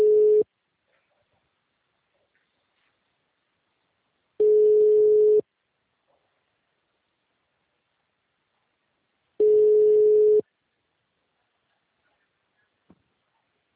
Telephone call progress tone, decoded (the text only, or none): ringback tone